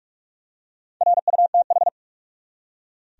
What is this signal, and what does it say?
Morse code, heard as a keyed tone, about 46 wpm, 700 Hz. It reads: RUTH
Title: USA Common Names - Max 5 Letters - 45wpm